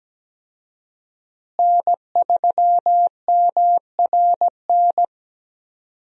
Morse code, heard as a keyed tone, about 17 wpm, 700 Hz. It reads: N3MRN
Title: USA Call Signs - 5 Characters - Repeated 3x - 17wpm